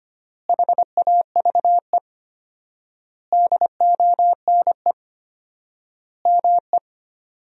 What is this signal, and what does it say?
Morse code, heard as a keyed tone, about 25 wpm, 700 Hz. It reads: HAVE DONE ME
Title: 4x Word-Spacing - Sentences from Top 500 Words - 25wpm